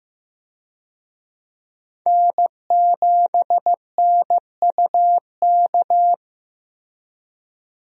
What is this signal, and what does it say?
Morse code, heard as a keyed tone, about 15 wpm, 700 Hz. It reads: N7NUK
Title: USA Call Signs - 5 Characters - Repeated 3x - 15wpm